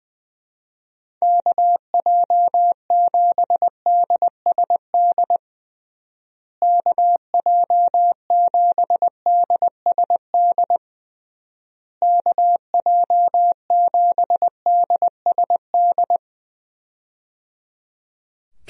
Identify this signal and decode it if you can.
Morse code, heard as a keyed tone, about 20 wpm, 700 Hz. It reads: KJ7DSD KJ7DSD KJ7DSD
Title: USA Call Signs - 6 Characters - Repeated 3x - 20wpm